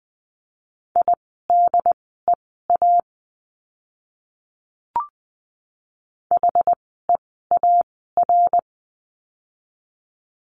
Morse code, heard as a keyed tone, about 20 wpm, spaced as for 14 wpm, 700 Hz. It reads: IDEA HEAR
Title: 108 Lesson 18f - Letter m - Words 20wpm